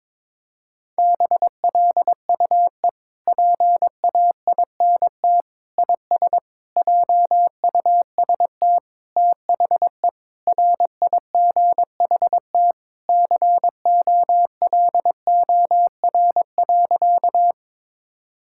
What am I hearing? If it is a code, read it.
Morse code, heard as a keyed tone, about 22 wpm, 700 Hz. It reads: BLUE PAINT IS JUST THE RIGHT COLOR.
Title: Sentences from Top 1000 Words - Review Part 6 - 22wpm